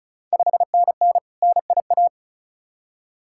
Morse code, heard as a keyed tone, about 36 wpm, 700 Hz. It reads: 5NN NIA
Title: Ontario QSO Party Exchange 35wpm